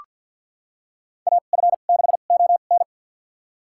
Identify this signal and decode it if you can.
Morse code, heard as a keyed tone, about 47 wpm, spaced as for 35 wpm, 700 Hz. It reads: AF6ON
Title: USA Call Signs - 5 Characters - 2x Character Spacing - 50wpm